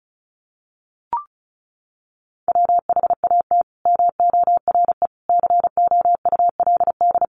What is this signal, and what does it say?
Morse code, heard as a keyed tone, about 34 wpm, 700 Hz. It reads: WHAT MORE COULD
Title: Sentences from Top 200 Words 35wpm